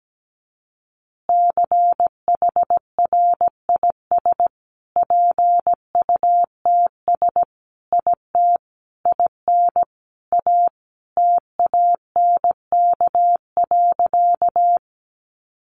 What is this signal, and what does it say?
Morse code, heard as a keyed tone, about 17 wpm, 700 Hz. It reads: CHRIS PUTS IT IN A TANK.
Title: Additional Easy Phrases 17wpm